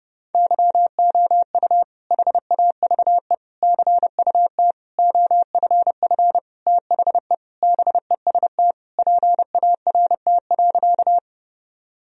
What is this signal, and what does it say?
Morse code, heard as a keyed tone, about 29 wpm, 700 Hz. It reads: YOU HAVE CUT OFF THE BEST PART.
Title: Sentences from Top 1000 Words - Review Part 1 - 30wpm